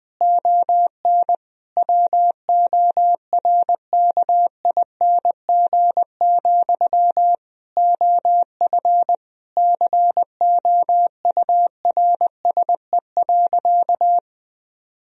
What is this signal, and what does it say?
Morse code, heard as a keyed tone, about 20 wpm, 700 Hz. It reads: ON WORKING, OF COURSE.
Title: Sentences from Top 1000 Words - Review Part 4 - 20wpm